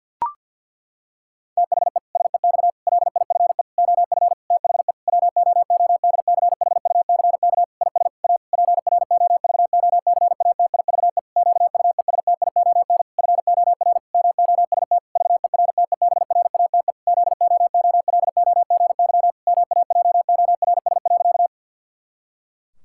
Morse code, heard as a keyed tone, about 49 wpm, 700 Hz. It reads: THE SEX LIFE OF THE WOODCHUCK IS A PROVOCATIVE QUESTION FOR MOST VERTEBRATE ZOOLOGY MAJORS.
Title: Pangram Challenge v2 - 50wpm